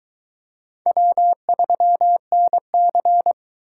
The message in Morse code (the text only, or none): W3NC